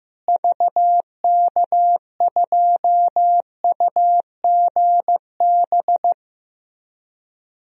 Morse code, heard as a keyed tone, about 15 wpm, 700 Hz. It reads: VK2UGB
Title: DXCC-100 Call Signs - 6 Characters - Repeated 3x - 15wpm